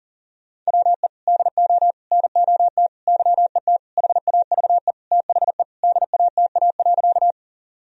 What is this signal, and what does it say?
Morse code, heard as a keyed tone, about 40 wpm, 700 Hz. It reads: WE DO NOT YET HAVE THE DATA.
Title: Sentences from Top 1000 Words - Review Part 3 - 40wpm